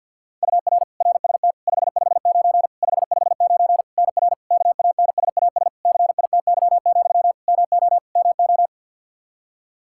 Morse code, heard as a keyed tone, about 51 wpm, 700 Hz. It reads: UR RST 559 559 NR KANSAS CITY, MO MO
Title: K1USN SST QSO Practice 50wpm